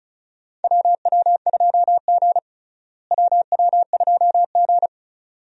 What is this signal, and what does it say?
Morse code, heard as a keyed tone, about 36 wpm, 700 Hz. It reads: WW2G WW2G
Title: USA Call Signs - 4 Characters - Repeated 2x - 35wpm